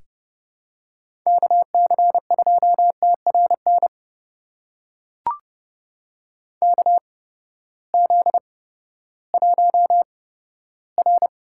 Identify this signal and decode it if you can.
Morse code, heard as a keyed tone, about 32 wpm, spaced as for 15 wpm, 700 Hz. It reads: KC2TRN KZ1R
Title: USA Call Signs - 8x Character Spacing - 30wpm